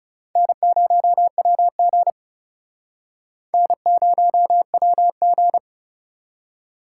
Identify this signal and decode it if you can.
Morse code, heard as a keyed tone, about 36 wpm, 700 Hz. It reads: N0WG N0WG
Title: Speed Racing - USA 1x2 Call Signs 30wpm